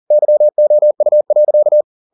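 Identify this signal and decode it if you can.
Morse code, heard as a keyed tone, about 40 wpm, 600 Hz. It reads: YOU.
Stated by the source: Morse code at 40 wpm, practice text, CW med Gnister.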